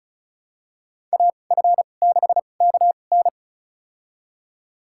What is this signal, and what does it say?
Morse code, heard as a keyed tone, about 36 wpm, spaced as for 25 wpm, 700 Hz. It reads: AF6KN
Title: USA Call Signs - 5 Characters - 2x Character Spacing - 35wpm